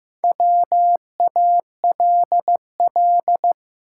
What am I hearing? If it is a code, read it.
Morse code, heard as a keyed tone, about 15 wpm, 700 Hz. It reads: WALL
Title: Sentences from Top 1100 Words - New Word Drill Alt - 15wpm